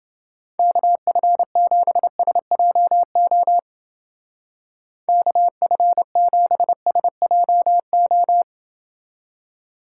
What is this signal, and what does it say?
Morse code, heard as a keyed tone, about 28 wpm, 700 Hz. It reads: KF7SJO KF7SJO